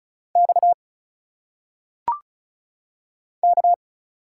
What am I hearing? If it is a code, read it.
Morse code, heard as a keyed tone, about 36 wpm, 700 Hz. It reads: X K
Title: Single Letters - Phonetic Pronunciation 35wpm